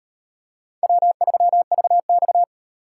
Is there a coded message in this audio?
Yes: Morse code from 0.8 to 2.4 s, heard as a keyed tone, about 38 wpm, 700 Hz. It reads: W3VX